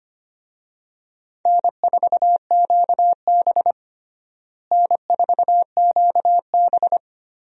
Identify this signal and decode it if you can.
Morse code, heard as a keyed tone, about 25 wpm, 700 Hz. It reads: N4QB N4QB
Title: USA Call Signs - 4 Characters - Repeated 2x - 25wpm